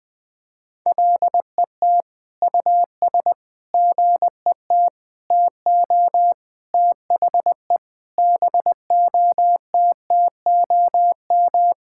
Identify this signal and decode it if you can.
Morse code, heard as a keyed tone, about 20 wpm, 700 Hz. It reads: LET US GET TO THE BOTTOM
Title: Sentences from Top 1000 Words - Review Part 6 - 20wpm